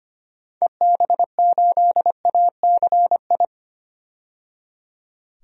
Morse code, heard as a keyed tone, about 25 wpm, 700 Hz. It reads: EB8ACI
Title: DXCC-100 Call Signs - 6 Characters - 25wpm